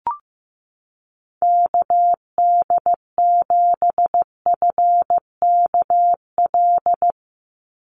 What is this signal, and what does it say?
Morse code, heard as a keyed tone, about 15 wpm, 700 Hz. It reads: KD7FKL
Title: USA Call Signs 15wpm